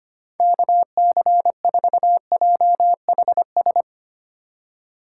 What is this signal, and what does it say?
Morse code, heard as a keyed tone, about 25 wpm, 700 Hz. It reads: KC4JHS
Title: USA Call Signs - 4x Character Spacing - 25wpm